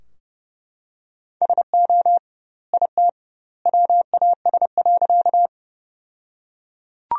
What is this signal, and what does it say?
Morse code, heard as a keyed tone, about 30 wpm, 700 Hz. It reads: SO IT WAS.
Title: 2x Word-Spacing - Sentences from Top 100 Words - 30wpm